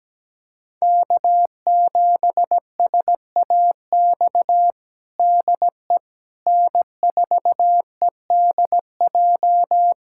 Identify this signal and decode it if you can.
Morse code, heard as a keyed tone, about 17 wpm, 700 Hz. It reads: K7SAX DE N4EDJ
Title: USA Callsign DE Callsign - 5 Characters - 17wpm